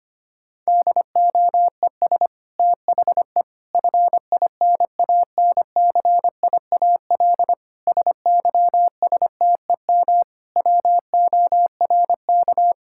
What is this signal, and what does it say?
Morse code, heard as a keyed tone, about 25 wpm, 700 Hz. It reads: DOES THE FINANCIAL SYSTEM WORK
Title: Sentences from Top 1000 Words - Review Part 5 - 25wpm